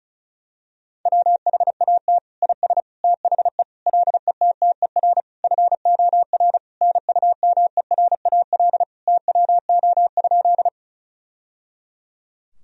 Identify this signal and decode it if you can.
Morse code, heard as a keyed tone, about 36 wpm, 700 Hz. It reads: WHAT IS THE LETTER FOR NUMERAL TWO?
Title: Sentences from Top 1000 Words - Review Part 2 - 35wpm